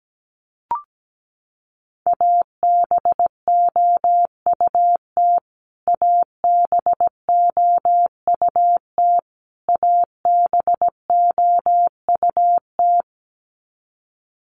Morse code, heard as a keyed tone, about 17 wpm, 700 Hz. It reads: ABOUT ABOUT ABOUT
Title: Top 300 Words Repeated 17wpm